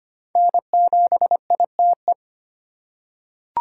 Morse code, heard as a keyed tone, about 25 wpm, 700 Hz. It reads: N7ITE
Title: USA Call Signs - 5 Characters - 4x Character Spacing - 25wpm